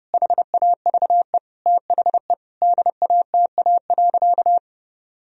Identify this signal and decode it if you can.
Morse code, heard as a keyed tone, about 31 wpm, 700 Hz. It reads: HAVE THE DATA.
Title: Sentences from Top 1000 Words - Review Part 3 - 30wpm